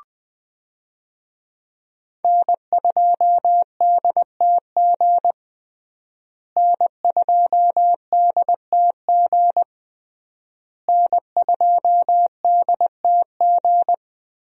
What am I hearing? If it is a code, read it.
Morse code, heard as a keyed tone, about 20 wpm, 700 Hz. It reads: N2DTG N2DTG N2DTG
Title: USA Call Signs - 5 Characters - Repeated 3x - 20wpm